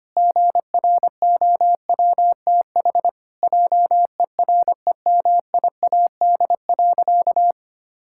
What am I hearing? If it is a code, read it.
Morse code, heard as a keyed tone, about 25 wpm, 700 Hz. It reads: GROWTH JEREMIAD.
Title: Pangram Challenge v2 - 25wpm